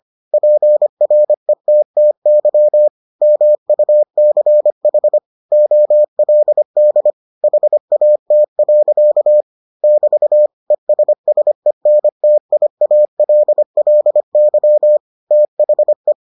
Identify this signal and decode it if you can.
Morse code, heard as a keyed tone, about 25 wpm, 600 Hz. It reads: PRETTY MUCH OLD HAT. <BT> ESSENTIALLY THE